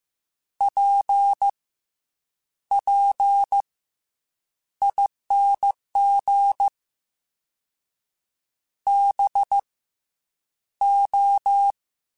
Morse code, heard as a keyed tone, about 15 wpm, spaced as for 8 wpm, 800 Hz. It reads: PPING BO
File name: Misc / Misc-mp3s / Jumps @13